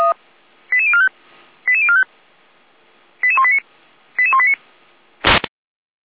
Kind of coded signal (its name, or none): five-tone selective call